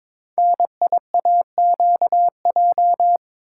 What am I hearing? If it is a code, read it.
Morse code, heard as a keyed tone, about 21 wpm, 700 Hz. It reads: NIAQJ
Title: Code Groups - 5 Letters - 22wpm